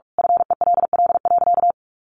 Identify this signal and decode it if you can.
Morse code, heard as a keyed tone, about 44 wpm, 700 Hz. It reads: FELL.
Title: A Sound Of Thunder 45wpm